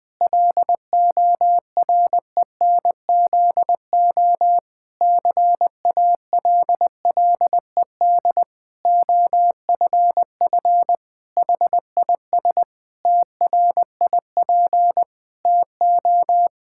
Morse code, heard as a keyed tone, about 20 wpm, 700 Hz. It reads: LORENZO CALLED OFF HIS TRIP TO
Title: Pangram Challenge v2 - 20wpm